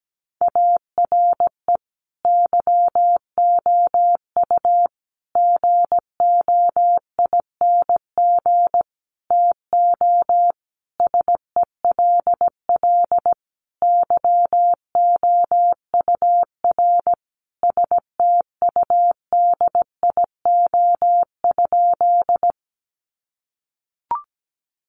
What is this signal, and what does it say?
Morse code, heard as a keyed tone, about 17 wpm, 700 Hz. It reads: ARE YOU GOING TO SELL YOUR STUDIO?
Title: Sentences from Top 1200 Words 17wpm